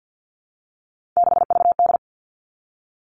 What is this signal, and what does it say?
Morse code, heard as a keyed tone, about 50 wpm, 700 Hz. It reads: 6VD
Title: All Cut Numbers - 3-Digit 50wpm